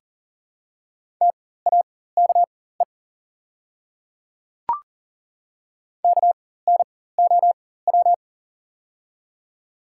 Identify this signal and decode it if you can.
Morse code, heard as a keyed tone, about 40 wpm, spaced as for 19 wpm, 700 Hz. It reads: TAKE KNOW
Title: 4x Character Spacing - Top 300 Words - 40wpm